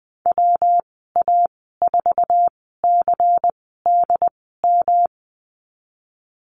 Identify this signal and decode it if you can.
Morse code, heard as a keyed tone, about 20 wpm, spaced as for 15 wpm, 700 Hz. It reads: WA4CDM
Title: 128 Lesson 21g - Number 6 - Call Signs 20wpm